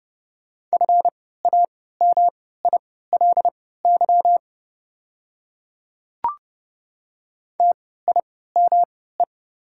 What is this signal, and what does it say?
Morse code, heard as a keyed tone, about 30 wpm, spaced as for 17 wpm, 690 Hz. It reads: FAMILY TIME
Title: 3x Character Spacing - Top 500 Words - 30wpm